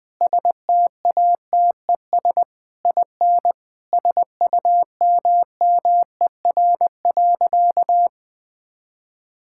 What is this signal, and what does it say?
Morse code, heard as a keyed tone, about 20 wpm, 700 Hz. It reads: STATES IN SUMMER.
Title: Sentences with 4-Word Phrases - 20wpm